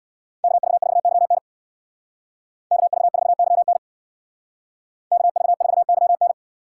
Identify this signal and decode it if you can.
Morse code, heard as a keyed tone, about 61 wpm, 700 Hz. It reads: KV4YN KV4YN KV4YN
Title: Speed Racing - USA 2x2 Call Signs 50wpm